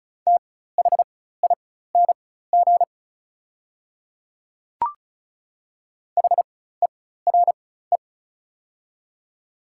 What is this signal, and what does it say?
Morse code, heard as a keyed tone, about 36 wpm, spaced as for 16 wpm, 690 Hz. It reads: THING HERE
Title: 4x Character Spacing - Top 500 Words - 35wpm